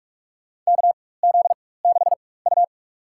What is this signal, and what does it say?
Morse code, heard as a keyed tone, about 46 wpm, spaced as for 21 wpm, 700 Hz. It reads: KZ6U
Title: USA Call Signs - 4x Character Spacing - 45wpm